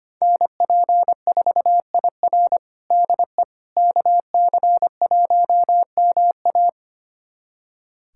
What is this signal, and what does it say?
Morse code, heard as a keyed tone, about 25 wpm, 700 Hz. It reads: NP4IR DE KC1MA
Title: USA Callsign DE Callsign - 5 Characters - 25wpm